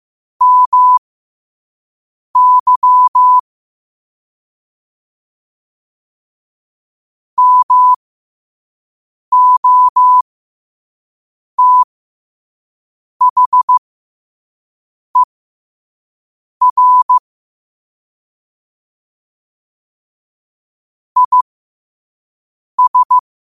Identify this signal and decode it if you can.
Morse code, heard as a keyed tone, about 15 wpm, spaced as for 5 wpm, 1000 Hz. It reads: MY MOTHER IS
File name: qotdfiles / 5wpm